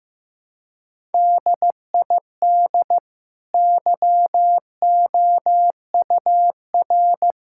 Morse code, heard as a keyed tone, about 15 wpm, 700 Hz. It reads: DID YOUR
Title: Sentences from Top 1000 Words - Review Part 6 - 15wpm